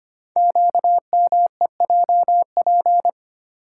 Morse code, heard as a keyed tone, about 25 wpm, 700 Hz. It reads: QMEJP
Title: Code Groups - 5 Letter:Numbers - 25wpm